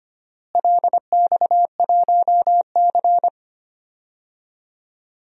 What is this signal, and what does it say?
Morse code, heard as a keyed tone, about 25 wpm, 700 Hz. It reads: LX1C
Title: DXCC-100 Call Signs - 4 Characters - Repeated 3x - 25wpm